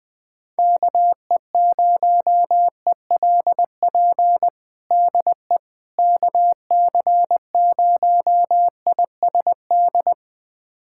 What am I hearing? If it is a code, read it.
Morse code, heard as a keyed tone, about 20 wpm, 700 Hz. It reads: KE0ELP DE KC0ISD
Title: USA Callsign DE Callsign - 6 Characters - 20wpm